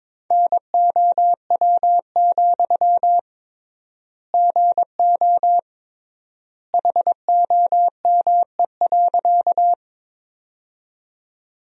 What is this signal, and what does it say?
Morse code, heard as a keyed tone, about 22 wpm, 700 Hz. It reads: NOW, GO HOME.
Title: 3x Word-Spacing - Sentences from Top 200 Words - 22wpm